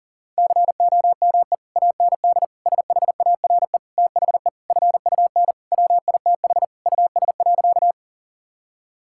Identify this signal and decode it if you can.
Morse code, heard as a keyed tone, about 40 wpm, 700 Hz. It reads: COME AND SHARE THE FUN WITH US.
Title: Sentences from Top 1000 Words - Review Part 6 - 40wpm